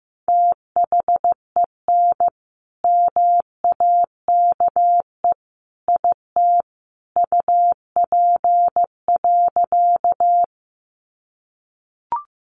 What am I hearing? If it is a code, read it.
Morse code, heard as a keyed tone, about 14 wpm, 700 Hz. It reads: THEN MAKE IT UP.
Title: Sentences from Top 100 Words 15wpm